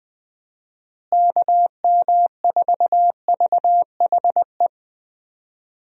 Morse code, heard as a keyed tone, about 21 wpm, 700 Hz. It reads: KM4VHE
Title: USA Call Signs - 6 Characters - Repeated 2x - 20wpm